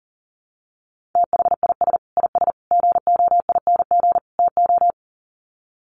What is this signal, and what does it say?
Morse code, heard as a keyed tone, about 40 wpm, 700 Hz. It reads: THIS IS GOING TO
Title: Sets of 4 Words 40wpm